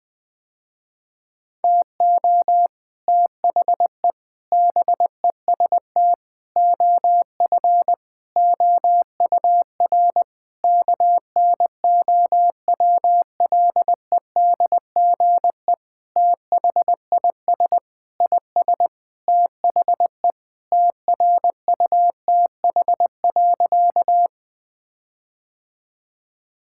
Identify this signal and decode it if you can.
Morse code, heard as a keyed tone, about 20 wpm, 700 Hz. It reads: TO THE BEST OF OUR KNOWLEDGE THIS IS THE TRUTH.
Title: Sentences with 9-Word Phrases - Spoken Beginning - 20wpm